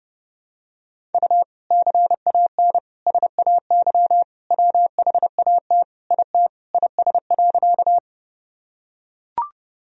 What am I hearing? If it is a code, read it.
Morse code, heard as a keyed tone, about 32 wpm, 700 Hz. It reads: U CAN SAY WHAT IT IS.
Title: Sentences from Top 200 Words and Abbreviations 30wpm